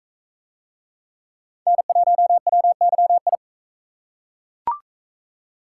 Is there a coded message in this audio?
Yes: Morse code from 1.7 to 3.4 s, heard as a keyed tone, about 42 wpm, 700 Hz. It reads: N1WYI